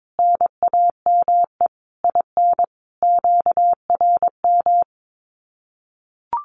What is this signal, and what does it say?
Morse code, heard as a keyed tone, about 22 wpm, 700 Hz. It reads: NAME IN QRM
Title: Common QSO Phrases 22wpm